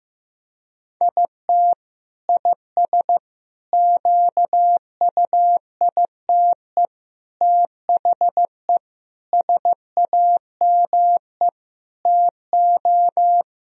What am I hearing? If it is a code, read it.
Morse code, heard as a keyed tone, about 15 wpm, 700 Hz. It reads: IT IS QUITE THE SAME TO
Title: Sentences from Top 1000 Words - Review Part 3 - 15wpm